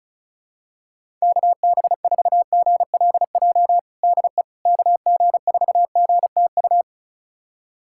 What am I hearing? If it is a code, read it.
Morse code, heard as a keyed tone, about 36 wpm, 700 Hz. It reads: KB4GLJ DE KG4GTU
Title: USA Callsign DE Callsign - 6 Characters - 35wpm